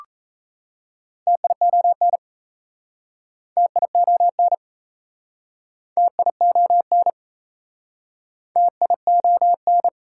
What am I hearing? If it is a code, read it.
Morse code, heard as a keyed tone, about 37 wpm, 700 Hz. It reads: TION TION TION TION